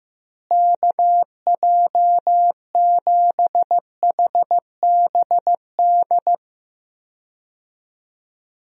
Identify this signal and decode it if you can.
Morse code, heard as a keyed tone, about 15 wpm, 700 Hz. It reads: KJ7HBD
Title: USA Call Signs - 6 Characters - Repeated 2x - 15wpm